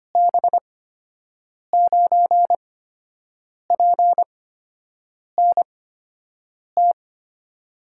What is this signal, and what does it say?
Morse code, heard as a keyed tone, about 25 wpm, spaced as for 7 wpm, 700 Hz. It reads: B9PNT
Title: USA Call Signs - 8x Character Spacing - 25wpm